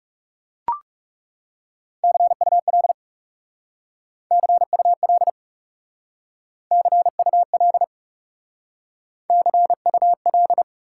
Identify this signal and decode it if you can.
Morse code, heard as a keyed tone, about 37 wpm, 700 Hz. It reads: CUL CUL CUL CUL
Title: Speed-Racing Top 90 QSO Elements 30wpm